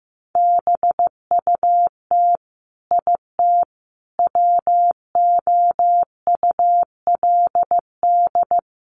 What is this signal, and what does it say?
Morse code, heard as a keyed tone, about 15 wpm, 700 Hz. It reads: BUT IT WOULD
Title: Sentences from Top 1500 Words 15wpm